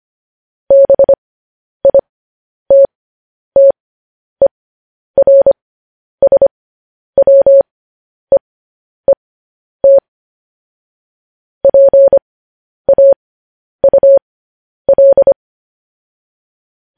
Morse code, heard as a keyed tone, about 25 wpm, spaced as for 10 wpm, 550 Hz. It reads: BITTERSWEET PAUL
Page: news-headlines-in-morse-code-at-10-wpm